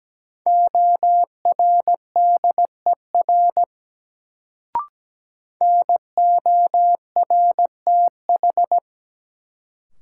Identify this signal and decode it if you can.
Morse code, heard as a keyed tone, about 17 wpm, 700 Hz. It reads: ORDER NORTH
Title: Sentences from Top 400 Words - New Word Drill - 17wpm